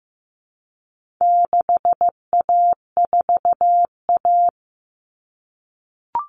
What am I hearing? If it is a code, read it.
Morse code, heard as a keyed tone, about 15 wpm, 700 Hz. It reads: 6A4A
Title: Common Cut Numbers - 4-Digit 15wpm